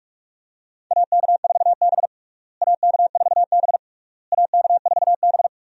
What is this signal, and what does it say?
Morse code, heard as a keyed tone, about 48 wpm, 700 Hz. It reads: AK4B AK4B AK4B
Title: USA Call Signs - 4 Characters - Repeated 3x - 45wpm